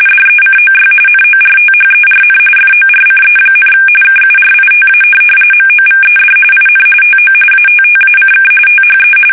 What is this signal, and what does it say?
RTTY, about 45.45 baud. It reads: DECOY NORTHERN FORCE, KURITA MOVED IN TO ATTACK THE
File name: fsk850 / USS-Massachusetts